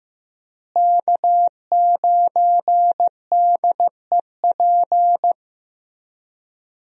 Morse code, heard as a keyed tone, about 15 wpm, 700 Hz. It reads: K9DEP
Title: USA Call Signs - 5 Characters - Repeated 3x - 15wpm